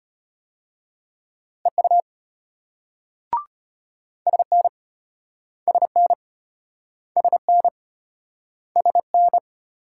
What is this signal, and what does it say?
Morse code, heard as a keyed tone, about 33 wpm, 700 Hz. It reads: EU SN SN SN SN